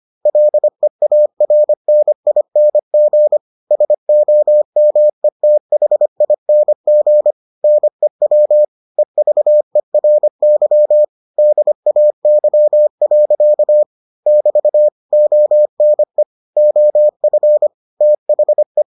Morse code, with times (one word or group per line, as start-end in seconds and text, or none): 0.3-3.4 LEARNING
3.7-7.3 SOMETHING
7.6-8.6 NEW
9.0-11.0 EVERY
11.4-13.8 DAY.
14.3-14.9 <BT>
15.1-16.2 ONE
16.6-17.7 OF
18.0-18.8 THE